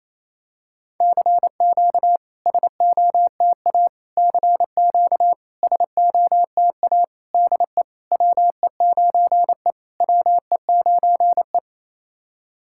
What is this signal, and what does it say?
Morse code, heard as a keyed tone, about 28 wpm, 700 Hz. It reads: CQ SOTA CQ SOTA DE WE9E WE9E
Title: SOTA QSO Practice 28wpm